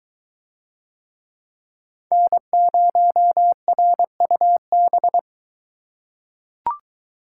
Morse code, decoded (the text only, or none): N0RUB